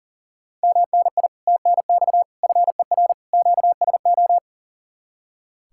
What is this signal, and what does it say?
Morse code, heard as a keyed tone, about 53 wpm, spaced as for 47 wpm, 700 Hz. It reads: MNI TNX FER QSO
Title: Speed-Racing Common QSO Phrases Encore - 40wpm